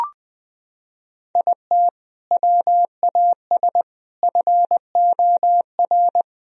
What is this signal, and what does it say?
Morse code, heard as a keyed tone, about 21 wpm, 700 Hz. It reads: IT WAS FOR
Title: Sentences from Top 1000 Words - Review Part 1 - 20wpm